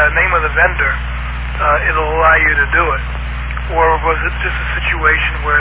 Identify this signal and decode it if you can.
Emergency Alert System (SAME) tones (header bursts), heard as harsh buzzing bursts, with speech. It is heard from 0.5 to 4.8 s.